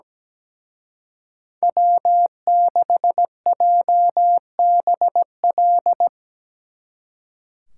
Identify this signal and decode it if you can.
Morse code, heard as a keyed tone, about 19 wpm, 700 Hz. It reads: W6JBL
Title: Speed Racing - USA 1x3 Call Signs 17wpm